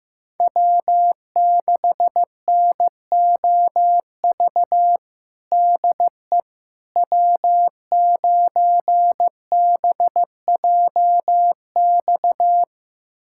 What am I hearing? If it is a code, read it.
Morse code, heard as a keyed tone, about 15 wpm, 700 Hz. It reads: W6NOV DE W9BJX
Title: USA Callsign DE Callsign - 5 Characters - 15wpm